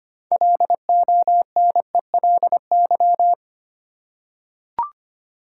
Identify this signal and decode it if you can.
Morse code, heard as a keyed tone, about 25 wpm, 700 Hz. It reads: LONELY
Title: Six Letter Words - 25wpm